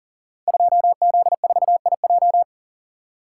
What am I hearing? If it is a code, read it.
Morse code, heard as a keyed tone, about 42 wpm, 700 Hz. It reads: 2Z4IJ
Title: Code Groups - 5 Letter:Numbers - 40wpm